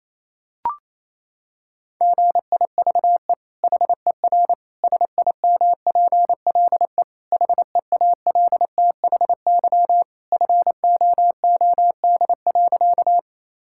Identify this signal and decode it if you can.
Morse code, heard as a keyed tone, about 28 wpm, 700 Hz. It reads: GIVE HER SIMPLE HEALTHY FOOD.
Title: Sentences from Top 1000 Words - Review Part 5 - 28wpm